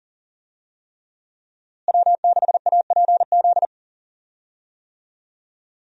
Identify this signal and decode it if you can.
Morse code, heard as a keyed tone, about 37 wpm, 700 Hz. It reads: W6APZ
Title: USA Call Signs - 5 Characters - Repeated 3x - 40wpm